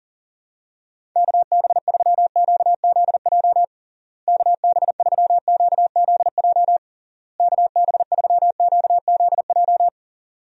Morse code, heard as a keyed tone, about 40 wpm, 700 Hz. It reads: KB3QZJ KB3QZJ KB3QZJ
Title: USA Call Signs - 6 Characters - Repeated 3x - 40wpm